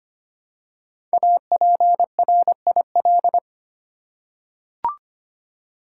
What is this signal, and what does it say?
Morse code, heard as a keyed tone, about 25 wpm, 700 Hz. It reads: APRIL
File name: 5 Letter Words - First 3 Letters Given - 25wpm